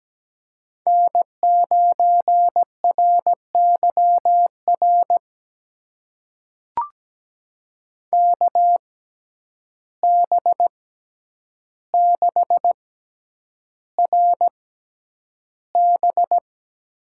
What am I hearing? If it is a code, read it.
Morse code, heard as a keyed tone, about 17 wpm, spaced as for 9 wpm, 700 Hz. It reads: N9RYR KB6RB
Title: USA Call Signs - 6x Character Spacing - 17wpm